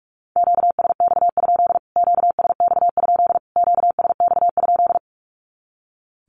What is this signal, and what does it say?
Morse code, heard as a keyed tone, about 44 wpm, 700 Hz. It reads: QSX? QSX? QSX?
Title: Q-Codes Repeated 3x 45wpm